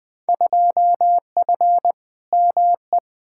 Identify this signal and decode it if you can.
Morse code, heard as a keyed tone, about 20 wpm, 700 Hz. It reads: 2F ME
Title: Field Day Exchanges - 20wpm